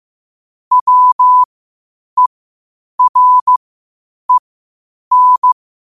Morse code, heard as a keyed tone, about 15 wpm, spaced as for 8 wpm, 1000 Hz. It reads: WEREN
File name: qotdfiles / 8wpm